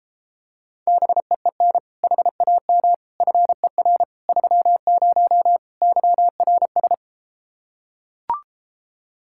Morse code, heard as a keyed tone, about 33 wpm, 700 Hz. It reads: BEEN HAM FER 30 YRS